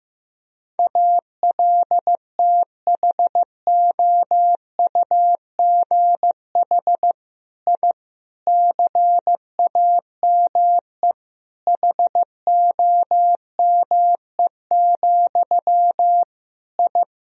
Morse code, heard as a keyed tone, about 15 wpm, 700 Hz. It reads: ALTHOUGH I CAME HOME, I
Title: Sentences from Top 1000 Words - Review Part 3 - 15wpm